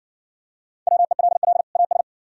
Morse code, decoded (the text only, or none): WELL AS